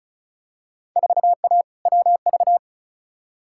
Morse code, heard as a keyed tone, about 36 wpm, 700 Hz. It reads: 4A WV
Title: Field Day Exchanges Encore - 35wpm